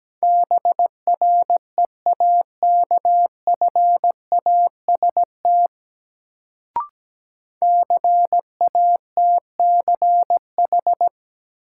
Morse code, heard as a keyed tone, about 17 wpm, 700 Hz. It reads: BREAKFAST CATCH
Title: Sentences from Top 1500 Words - New Word Drill - 17wpm